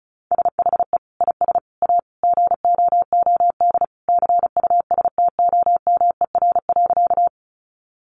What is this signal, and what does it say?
Morse code, heard as a keyed tone, about 34 wpm, 700 Hz. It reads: SHE IS A GOOD CUSTOMER.
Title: Sentences from Top 1100 Words 35wpm